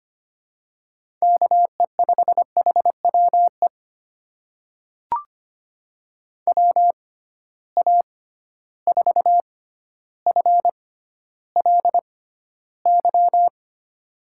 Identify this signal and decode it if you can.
Morse code, heard as a keyed tone, about 25 wpm, spaced as for 13 wpm, 700 Hz. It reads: KE5HWE WA4FLY
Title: USA Call Signs - 6 Characters - 6x Character Spacing - 25wpm